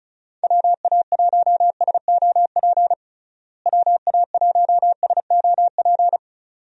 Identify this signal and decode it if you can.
Morse code, heard as a keyed tone, about 36 wpm, 700 Hz. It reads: WA1SOP WA1SOP
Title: USA Call Signs - 6 Characters - Repeated 2x - 35wpm